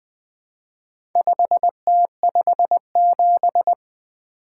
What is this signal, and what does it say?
Morse code, heard as a keyed tone, about 20 wpm, 700 Hz. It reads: 5T57
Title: Four Letter-Number Combinations - 20wpm